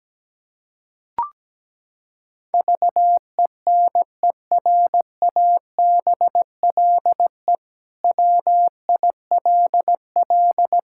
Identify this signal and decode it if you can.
Morse code, heard as a keyed tone, about 17 wpm, 700 Hz. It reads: VENERABLE WILL
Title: Pangram Challenge v2 - 17wpm